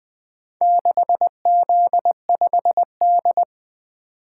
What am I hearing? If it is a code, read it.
Morse code, heard as a keyed tone, about 20 wpm, 700 Hz. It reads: 6Z5D
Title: Four Letter-Number Combinations - 20wpm